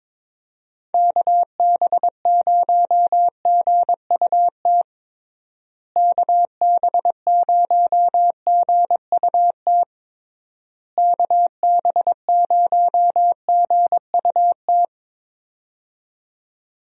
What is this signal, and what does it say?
Morse code, heard as a keyed tone, about 22 wpm, 700 Hz. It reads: KB0GUT KB0GUT KB0GUT
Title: USA Call Signs - 6 Characters - Repeated 3x - 22wpm